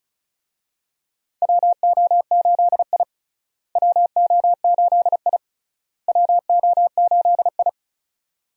Morse code, heard as a keyed tone, about 34 wpm, 700 Hz. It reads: WO8I WO8I WO8I
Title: USA Call Signs - 4 Characters - Repeated 3x - 35wpm